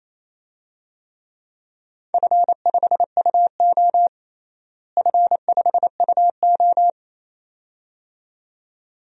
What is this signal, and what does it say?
Morse code, heard as a keyed tone, about 28 wpm, 700 Hz. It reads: F5UO F5UO
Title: DXCC-100 Call Signs - 4 Characters - Repeated 2x - 28wpm